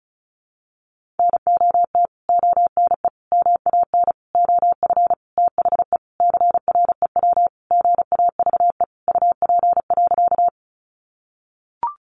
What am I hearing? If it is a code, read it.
Morse code, heard as a keyed tone, about 34 wpm, 700 Hz. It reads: NOT ONE MAN OF THE CREW GAVE UP.
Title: Sentences from Top 1500 Words 35wpm